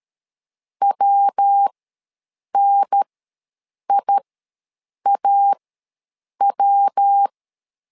Morse code, heard as a keyed tone, about 13 wpm, spaced as for 7 wpm, 800 Hz. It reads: WNIAW
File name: LEZIONI / 3 MWRO / 3 MW ws 12wpm 1816